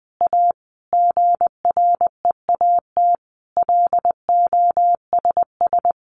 Morse code, heard as a keyed tone, about 20 wpm, 700 Hz. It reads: A GREAT LOSS
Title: Sentences from Top 1400 Words 20wpm